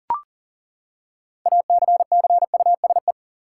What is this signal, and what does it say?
Morse code, heard as a keyed tone, about 40 wpm, 700 Hz. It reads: ACCUSE
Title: Six Letter Words - 40wpm